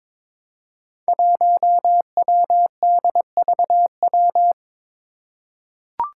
Morse code, heard as a keyed tone, about 21 wpm, 700 Hz. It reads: 1WDVW
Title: Five Challenging Letter-Number Combinations 22wpm